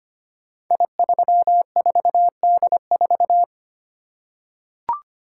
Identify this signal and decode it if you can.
Morse code, heard as a keyed tone, about 25 wpm, 700 Hz. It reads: I34D4
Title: Code Groups - 5 Letter:Numbers - 25wpm